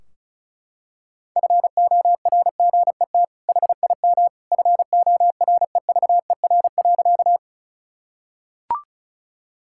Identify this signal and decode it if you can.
Morse code, heard as a keyed tone, about 36 wpm, 700 Hz. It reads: FORGET HIM FOREVER.
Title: Sentences from Top 1600 Words 35wpm